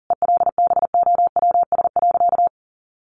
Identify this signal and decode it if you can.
Morse code, heard as a keyed tone, about 40 wpm, 700 Hz. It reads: ELBOWS.
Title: A Sound Of Thunder 40wpm